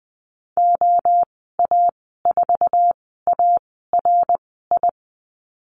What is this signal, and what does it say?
Morse code, heard as a keyed tone, about 20 wpm, spaced as for 15 wpm, 700 Hz. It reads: OA4ARI
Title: 082 Lesson 14g - Number 2 - Call Signs 20wpm